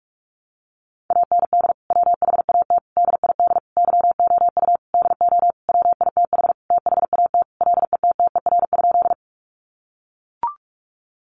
Morse code, heard as a keyed tone, about 44 wpm, 700 Hz. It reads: AND WHAT DID YOU DO WITH THAT LETTER?
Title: Sentences from Top 300 Words 45wpm